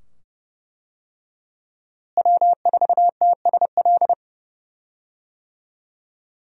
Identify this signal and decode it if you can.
Morse code, heard as a keyed tone, about 32 wpm, 700 Hz. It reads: W4TSL
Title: USA Call Signs - 5 Characters - Repeated 2x - 30wpm